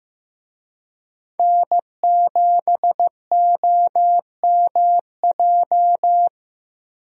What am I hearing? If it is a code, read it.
Morse code, heard as a keyed tone, about 15 wpm, 700 Hz. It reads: N7OMJ
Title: USA Call Signs - 5 Characters - Repeated 2x - 15wpm